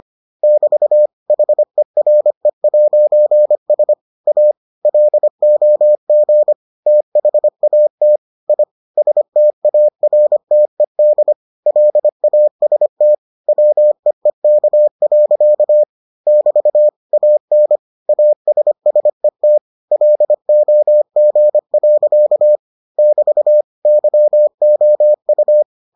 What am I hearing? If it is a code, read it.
Morse code, heard as a keyed tone, about 25 wpm, 600 Hz. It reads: <BT> HERE'S A LOG THAT I STARTED LAST WEEK. <BT> AN ASSET LOG. <BT> YOU